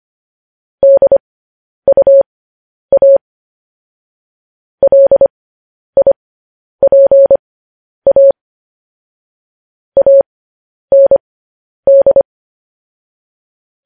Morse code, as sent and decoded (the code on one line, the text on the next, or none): -.. ..- .- / .-.. .. .--. .- / .- -. -..
DUA LIPA AND